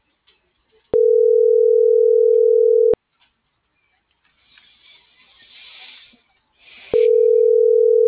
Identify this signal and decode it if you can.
Ringback tone (North America), 440 + 480 Hz.
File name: CENTRAL MAERSK / out isabel.bravo - .81988